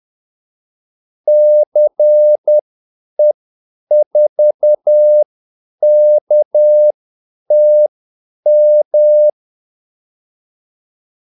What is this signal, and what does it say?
Morse code, heard as a keyed tone, about 10 wpm, spaced as for 8 wpm, 600 Hz. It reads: CE4KTM
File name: Callsigns-10-8wpm